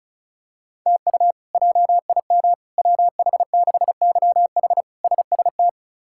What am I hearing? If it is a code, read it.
Morse code, heard as a keyed tone, about 34 wpm, 700 Hz. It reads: TU JIM WH6YH SST
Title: K1USN SST QSO Practice - 35wpm